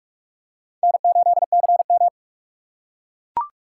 Morse code, heard as a keyed tone, about 44 wpm, 700 Hz. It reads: N8CM